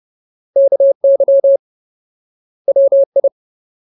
Morse code, heard as a keyed tone, about 30 wpm, 550 Hz. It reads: KY WI
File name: Ultimate Word List / 30WPM Ultimate